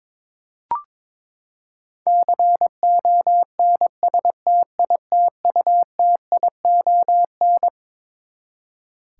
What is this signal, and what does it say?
Morse code, heard as a keyed tone, about 22 wpm, 700 Hz. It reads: CONSTITUTION
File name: Twelve Letter Words - 22wpm